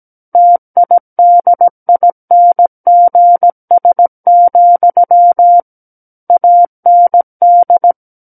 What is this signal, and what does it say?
Morse code, heard as a keyed tone, about 17 wpm, 700 Hz. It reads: TIDINGS, AND
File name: Book One / 08-Chapter 8 - 17 WPM